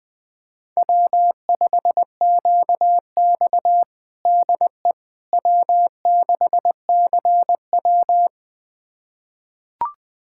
Morse code, heard as a keyed tone, about 21 wpm, 700 Hz. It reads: W5QX DE W6CW
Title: USA Callsign DE Callsign - 4 Characters - 20wpm